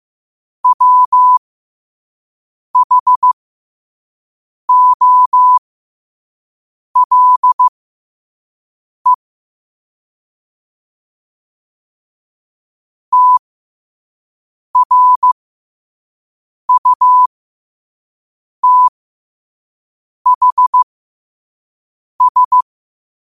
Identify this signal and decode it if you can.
Morse code, heard as a keyed tone, about 15 wpm, spaced as for 5 wpm, 1000 Hz. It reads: WHOLE TRUTHS
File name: qotdfiles / 5wpm